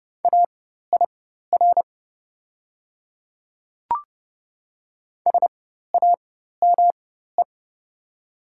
Morse code, heard as a keyed tone, about 32 wpm, spaced as for 14 wpm, 700 Hz. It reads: AIR SAME
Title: 4x Character Spacing - Top 200 Words - 30wpm